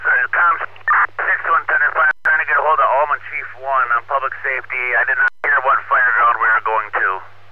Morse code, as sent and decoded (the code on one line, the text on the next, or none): none